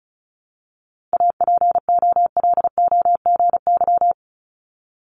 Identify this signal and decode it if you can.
Morse code, heard as a keyed tone, about 36 wpm, 700 Hz. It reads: APOLOGY
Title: Multi-letter Phonemes Repeated 3x 35wpm